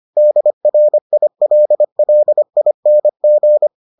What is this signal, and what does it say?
Morse code, heard as a keyed tone, about 25 wpm, 600 Hz. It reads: DRILLING